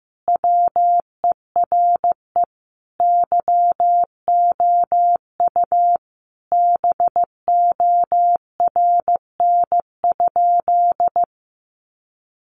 Morse code, heard as a keyed tone, about 15 wpm, 700 Hz. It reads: WERE YOU BORN?
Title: Sentences from Top 1000 Words 15wpm